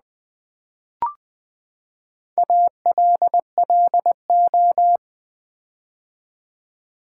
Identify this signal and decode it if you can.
Morse code, heard as a keyed tone, about 20 wpm, 700 Hz. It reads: ALLO
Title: Build A Word - 4 Letters At A Time - 8-Letter Words - 20wpm